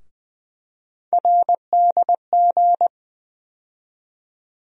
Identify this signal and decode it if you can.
Morse code, heard as a keyed tone, about 20 wpm, 700 Hz. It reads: RDG
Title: USA Call Sign Postfixes - 3 Letters 20wpm